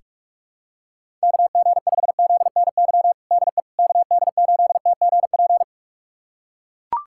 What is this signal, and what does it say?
Morse code, heard as a keyed tone, about 43 wpm, 700 Hz. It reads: KG5ZNY DE KD8TGP
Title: USA Callsign DE Callsign - 6 Characters - 45wpm